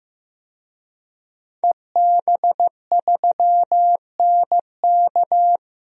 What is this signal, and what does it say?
Morse code, heard as a keyed tone, about 15 wpm, 700 Hz. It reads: EB3NK
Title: DXCC-100 Call Signs - 5 Characters - 15wpm